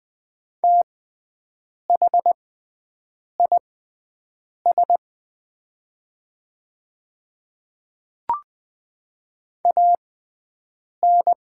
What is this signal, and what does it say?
Morse code, heard as a keyed tone, about 20 wpm, spaced as for 7 wpm, 700 Hz. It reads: THIS AN
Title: 6x Character Spacing - Top 200 Words - 20wpm